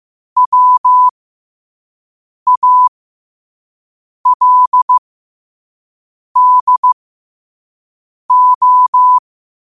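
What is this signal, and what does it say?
Morse code, heard as a keyed tone, about 15 wpm, spaced as for 5 wpm, 1000 Hz. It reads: WALDO
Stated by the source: Morse code quote of the day at 5 wpm.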